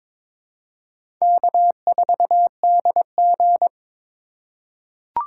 Morse code, heard as a keyed tone, about 22 wpm, 700 Hz. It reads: K4DG
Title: USA Call Signs - 4 Characters - 2x Character Spacing - 22wpm